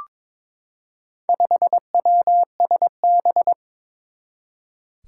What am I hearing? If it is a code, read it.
Morse code, heard as a keyed tone, about 22 wpm, 700 Hz. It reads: 5WSB
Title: Four Challenging Letter-Number Combinations 22wpm